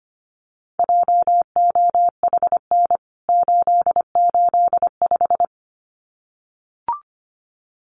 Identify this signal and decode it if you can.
Morse code, heard as a keyed tone, about 25 wpm, 700 Hz. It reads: JOHN 885
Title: CWT Exchange 25wpm